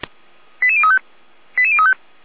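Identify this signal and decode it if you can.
Five-tone selective call (ZVEI), heard as quick runs of five tones, tones of about 70 ms. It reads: 88035 88035
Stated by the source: ZVEI five-tone selective call received by SDR, recorded on the feuerwehr-orlamuende.org fire brigade audio server.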